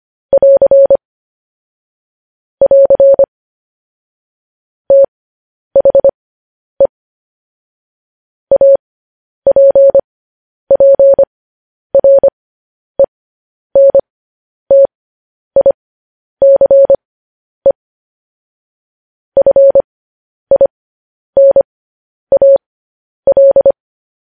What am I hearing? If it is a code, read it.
Morse code, heard as a keyed tone, about 25 wpm, spaced as for 10 wpm, 550 Hz. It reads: <AR> <AR> THE APPRENTICE FINAL
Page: news headlines in morse code at 10 wpm